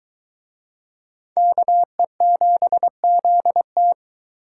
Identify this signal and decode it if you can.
Morse code, heard as a keyed tone, about 24 wpm, 700 Hz. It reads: KE7ZT